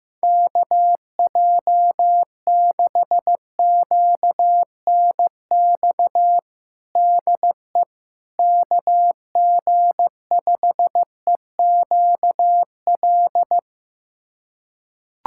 Morse code, heard as a keyed tone, about 15 wpm, 700 Hz. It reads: KJ6QNX DE KG5EQL
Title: USA Callsign DE Callsign - 6 Characters - 15wpm